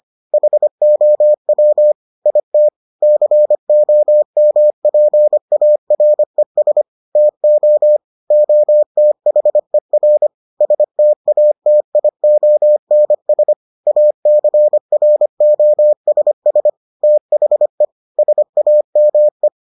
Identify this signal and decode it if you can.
Morse code, heard as a keyed tone, about 25 wpm, 600 Hz. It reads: HOW IT COMPARES TO OTHER STATIONS ACROSS THE SAME